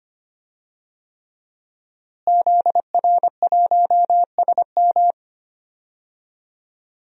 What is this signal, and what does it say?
Morse code, heard as a keyed tone, about 25 wpm, 700 Hz. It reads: ZR1SM
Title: DXCC-100 Call Signs - 5 Characters - Repeated 3x - 25wpm